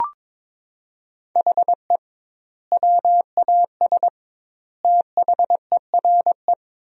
Morse code, heard as a keyed tone, about 22 wpm, 700 Hz. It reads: HE WAS THERE
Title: 2x Word-Spacing - Sentences from Top 200 Words - 22wpm